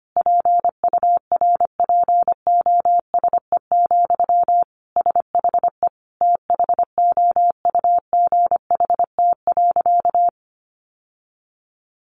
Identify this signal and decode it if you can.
Morse code, heard as a keyed tone, about 25 wpm, 700 Hz. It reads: PURPOSE, SHE THOUGHT.
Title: Sentences from Top 800 Words 25wpm